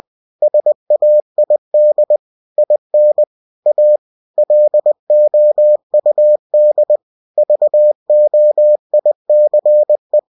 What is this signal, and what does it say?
Morse code, heard as a keyed tone, about 20 wpm, 600 Hz. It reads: SAID IN A LOUD VOICE
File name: cw-books / winnie-the-pooh / winnie-20wpm-page-0069